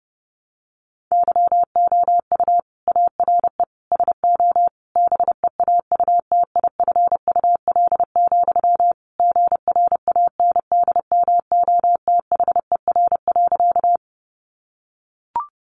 Morse code, heard as a keyed tone, about 30 wpm, 700 Hz. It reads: YOU ARE SO BEAUTIFUL, GRANDMOTHER.
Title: Sentences from Top 1200 Words 30wpm